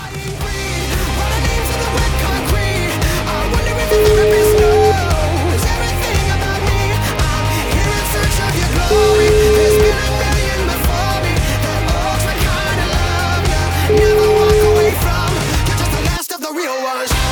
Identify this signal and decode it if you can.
Ringback tone (Europe, 425 Hz), with noise.